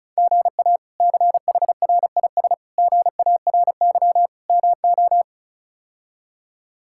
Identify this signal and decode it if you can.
Morse code, heard as a keyed tone, about 36 wpm, 700 Hz. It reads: GA CHRIS GARY MO
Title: K1USN SST QSO Practice - 35wpm